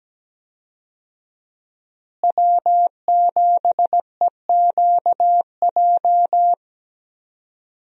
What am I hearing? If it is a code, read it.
Morse code, heard as a keyed tone, about 17 wpm, 700 Hz. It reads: W7EQJ
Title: USA Call Signs - 5 Characters - Repeated 2x - 17wpm